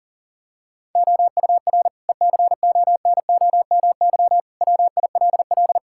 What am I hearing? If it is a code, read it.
Morse code, heard as a keyed tone, about 40 wpm, 690 Hz. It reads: OUR ECONOMY WILL
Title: Sentences from Top 1000 Words - Review Part 5 - 40wpm